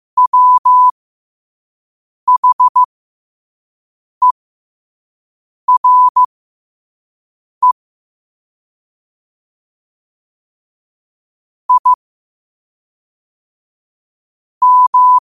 Morse code, heard as a keyed tone, about 15 wpm, spaced as for 5 wpm, 1000 Hz. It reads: WHERE I M
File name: qotdfiles / 5wpm